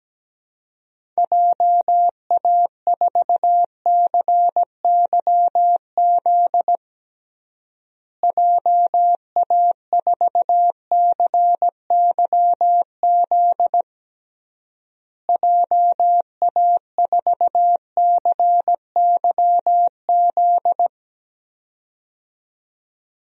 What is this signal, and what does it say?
Morse code, heard as a keyed tone, about 17 wpm, 700 Hz. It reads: JA4CYZ JA4CYZ JA4CYZ
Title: DXCC-100 Call Signs - 6 Characters - Repeated 3x - 17wpm